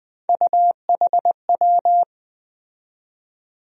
Morse code, heard as a keyed tone, about 20 wpm, 700 Hz. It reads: UHW
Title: USA Call Sign Postfixes - 3 Letters 20wpm